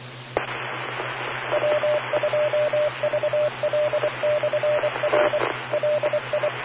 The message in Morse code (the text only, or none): E W2VL/R LI